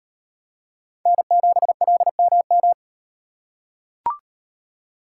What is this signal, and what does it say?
Morse code, heard as a keyed tone, about 38 wpm, 700 Hz. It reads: N7LMM